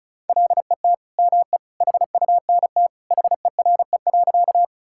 Morse code, heard as a keyed tone, about 36 wpm, 700 Hz. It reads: LET ME HUNT HERE.
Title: Sentences from Top 1600 Words 35wpm